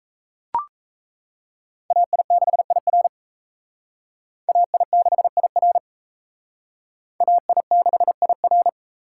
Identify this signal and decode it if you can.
Morse code, heard as a keyed tone, about 34 wpm, 700 Hz. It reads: AI6IR AI6IR AI6IR
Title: Speed Racing - USA 2x2 Call Signs 28wpm